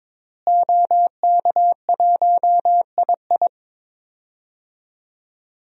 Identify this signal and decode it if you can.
Morse code, heard as a keyed tone, about 21 wpm, 700 Hz. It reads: OK1II
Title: DXCC-100 Call Signs - 5 Characters - Repeated 3x - 22wpm